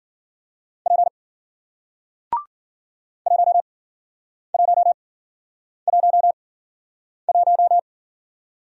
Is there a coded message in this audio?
Yes: Morse code from 0.9 to 7.8 s, heard as a keyed tone, about 54 wpm, spaced as for 9 wpm, 700 Hz. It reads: P 1111